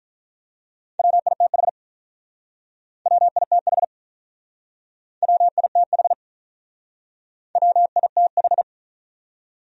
Morse code, heard as a keyed tone, about 47 wpm, 700 Hz. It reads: WITH WITH WITH WITH